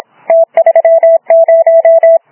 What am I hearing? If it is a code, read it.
Morse code, heard as a keyed tone, about 28 wpm, 660 Hz. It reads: T30
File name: Archiv / CUC318 / CUC318 záznam OK1DCS